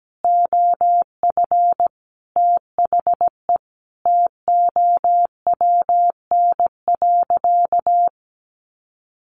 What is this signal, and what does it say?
Morse code, heard as a keyed tone, about 17 wpm, 700 Hz. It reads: OF THE TOWN.
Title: Sentences from Top 1400 Words 17wpm